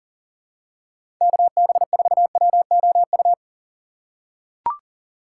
Morse code, heard as a keyed tone, about 40 wpm, 700 Hz. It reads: KB4WOU
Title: USA Call Signs - 6 Characters - 2x Character Spacing - 40wpm